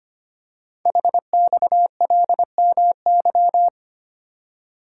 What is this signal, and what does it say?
Morse code, heard as a keyed tone, about 25 wpm, 700 Hz. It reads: HXLMY
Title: Code Groups - 5 Letters - 25wpm